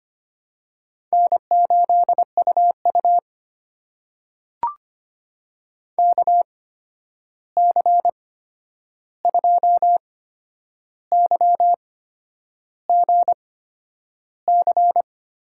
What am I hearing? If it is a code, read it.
Morse code, heard as a keyed tone, about 25 wpm, spaced as for 9 wpm, 700 Hz. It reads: N8UU KC2YGC
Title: USA Call Signs - 8x Character Spacing - 25wpm